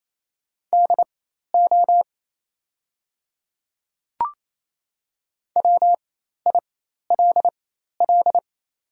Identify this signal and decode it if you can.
Morse code, heard as a keyed tone, about 28 wpm, spaced as for 13 wpm, 700 Hz. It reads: DO WILL
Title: 4x Character Spacing - Top 200 Words - 28wpm